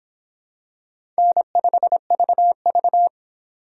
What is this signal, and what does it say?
Morse code, heard as a keyed tone, about 26 wpm, 700 Hz. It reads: N5VV